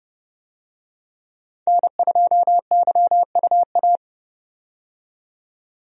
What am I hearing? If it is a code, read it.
Morse code, heard as a keyed tone, about 32 wpm, 700 Hz. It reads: N2YUA
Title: USA Call Signs - 5 Characters - 30wpm